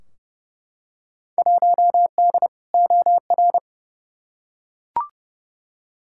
Morse code, heard as a keyed tone, about 30 wpm, 700 Hz. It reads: 1D OR
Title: Field Day Exchanges - 30wpm